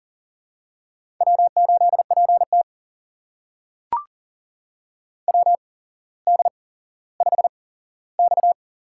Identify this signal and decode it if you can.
Morse code, heard as a keyed tone, about 38 wpm, spaced as for 17 wpm, 700 Hz. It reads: W8PT WD5X
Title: USA Call Signs - 4 Characters - 8x Character Spacing - 40wpm